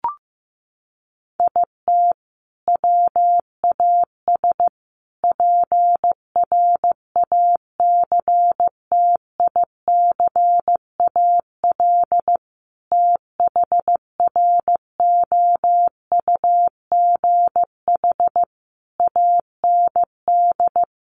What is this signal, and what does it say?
Morse code, heard as a keyed tone, about 15 wpm, 700 Hz. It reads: IT WAS PRACTICAL THROUGH AND
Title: Sentences from Top 1400 Words 15wpm